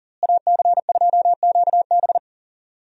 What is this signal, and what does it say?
Morse code, heard as a keyed tone, about 40 wpm, 700 Hz. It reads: AC2QB
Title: USA Call Signs - 5 Characters - Repeated 2x - 40wpm